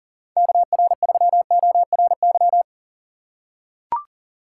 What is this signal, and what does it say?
Morse code, heard as a keyed tone, about 40 wpm, 700 Hz. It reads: KR3ORY
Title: USA Call Signs - 6 Characters - 4x Character Spacing - 40wpm